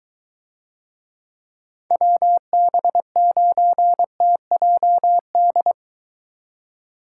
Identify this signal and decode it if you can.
Morse code, heard as a keyed tone, about 22 wpm, 700 Hz. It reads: WB9TJD